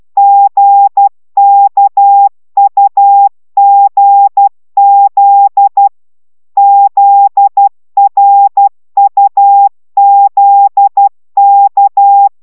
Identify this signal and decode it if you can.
Morse code, heard as a keyed tone, about 12 wpm, 800 Hz. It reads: GKUGZ ZRUZK
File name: taviro tanulas / cw 69 15 10